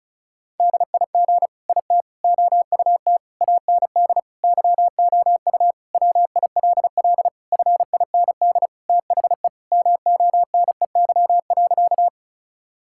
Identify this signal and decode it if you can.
Morse code, heard as a keyed tone, about 34 wpm, 700 Hz. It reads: DIG IT OUT AND YOU WILL FIND THE MONEY.
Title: Sentences from Top 1600 Words 35wpm